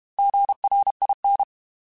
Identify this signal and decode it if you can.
Morse code, heard as a keyed tone, about 32 wpm, 800 Hz. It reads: GRIN
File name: Alphabetical / Alphabetical mp3s / Gs @30